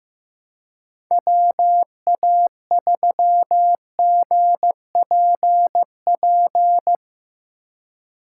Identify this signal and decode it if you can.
Morse code, heard as a keyed tone, about 16 wpm, 700 Hz. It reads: WA3GPP
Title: K1USN SST QSO Practice - 15wpm